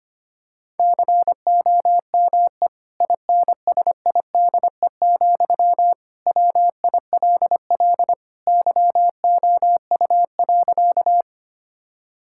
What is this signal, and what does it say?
Morse code, heard as a keyed tone, about 25 wpm, 700 Hz. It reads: COME INSIDE, WILL YOU.
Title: Sentences from Top 1000 Words - Review Part 6 - 25wpm